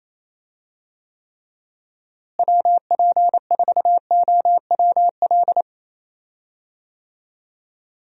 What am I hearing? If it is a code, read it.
Morse code, heard as a keyed tone, about 28 wpm, 700 Hz. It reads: WP4OWL
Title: USA Call Signs - 6 Characters - Repeated 3x - 28wpm